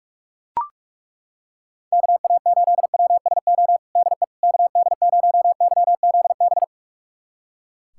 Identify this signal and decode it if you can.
Morse code, heard as a keyed tone, about 41 wpm, 700 Hz. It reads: KA8WSO DE KD0YZB
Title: USA Callsign DE Callsign - 6 Characters - 45wpm